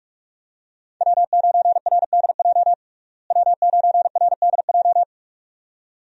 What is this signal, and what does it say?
Morse code, heard as a keyed tone, about 46 wpm, 700 Hz. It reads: W9RDJ W9RDJ
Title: USA Call Signs - 5 Characters - Repeated 2x - 45wpm